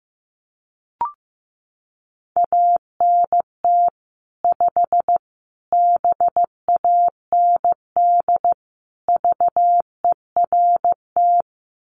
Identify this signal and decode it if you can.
Morse code, heard as a keyed tone, about 14 wpm, 700 Hz. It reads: ANT 5 BAND VERT
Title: Common QSO Phrases 15wpm